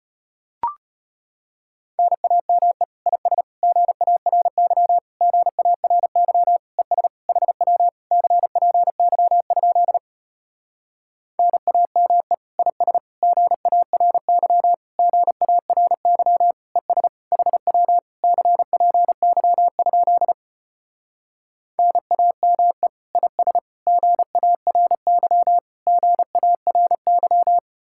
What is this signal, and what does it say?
Morse code, heard as a keyed tone, about 33 wpm, 700 Hz. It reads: NAME IS GARY GARY ES HW CPY? NAME IS GARY GARY ES HW CPY? NAME IS GARY GARY